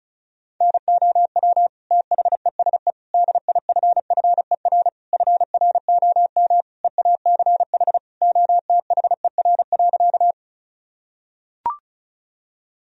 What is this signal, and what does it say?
Morse code, heard as a keyed tone, about 34 wpm, 700 Hz. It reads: NOW THESE DIFFER FROM EACH OTHER.
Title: Sentences from Top 1000 Words - Review Part 1 - 35wpm